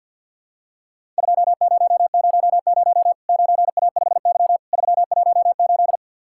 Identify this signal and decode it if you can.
Morse code, heard as a keyed tone, about 50 wpm, 700 Hz. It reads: 2000 9A5Y 318
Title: Worked All Europe DX Contest - QTC Exchange - 50wpm